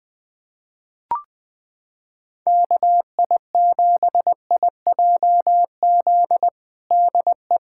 Morse code, heard as a keyed tone, about 21 wpm, 700 Hz. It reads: KI7IJZ DE
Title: USA Callsign DE Callsign 20wpm